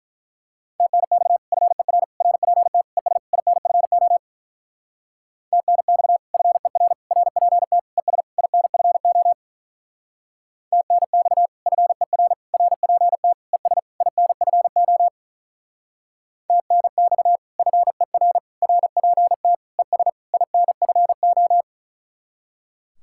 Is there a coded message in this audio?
Yes: Morse code from 0.8 to 21.6 s, heard as a keyed tone, about 44 wpm, 700 Hz. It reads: TNX FER RPT ES INFO TNX FER RPT ES INFO TNX FER RPT ES INFO TNX FER RPT ES INFO